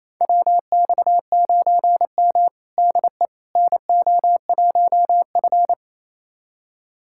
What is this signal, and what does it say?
Morse code, heard as a keyed tone, about 28 wpm, 700 Hz. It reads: WX9M DE NO1F
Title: USA Callsign DE Callsign - 4 Characters - 28wpm